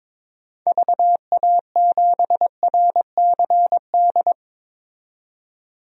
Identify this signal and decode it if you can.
Morse code, heard as a keyed tone, about 21 wpm, 700 Hz. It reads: VA7RCD
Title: DXCC-100 Call Signs - 6 Characters - 22wpm